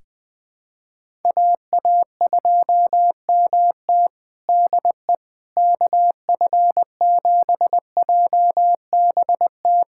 Morse code, heard as a keyed tone, about 20 wpm, 700 Hz. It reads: AA2MT DE KF7JBT
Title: USA Callsign DE Callsign 20wpm